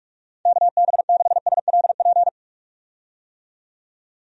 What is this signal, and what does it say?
Morse code, heard as a keyed tone, about 46 wpm, 700 Hz. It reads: KB6SLP
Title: USA Call Signs - 6 Characters - 45wpm